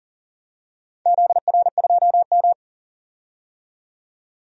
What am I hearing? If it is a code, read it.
Morse code, heard as a keyed tone, about 38 wpm, 700 Hz. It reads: ZR2M
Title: DXCC-100 Call Signs - 4 Characters - 40wpm